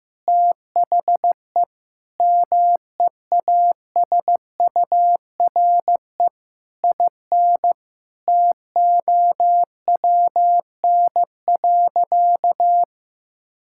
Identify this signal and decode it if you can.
Morse code, heard as a keyed tone, about 15 wpm, 700 Hz. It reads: THE MEASURE IN TOWN.
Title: Sentences from Top 1000 Words - Review Part 5 - 15wpm